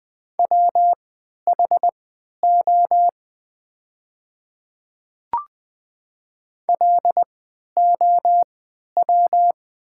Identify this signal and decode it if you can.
Morse code, heard as a keyed tone, about 20 wpm, spaced as for 11 wpm, 700 Hz. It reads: WHO LOW
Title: 3x Character Spacing - Top 300 Words - 20wpm